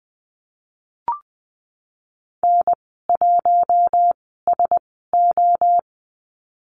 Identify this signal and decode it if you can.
Morse code, heard as a keyed tone, about 20 wpm, spaced as for 14 wpm, 700 Hz. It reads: N1SO
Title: 122 Lesson 20g - Number 3 - Call Signs 20wpm